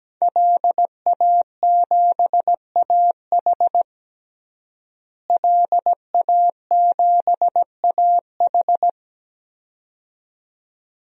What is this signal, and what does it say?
Morse code, heard as a keyed tone, about 17 wpm, 700 Hz. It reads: LA7AH LA7AH
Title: DXCC-100 Call Signs - 5 Characters - Repeated 2x - 17wpm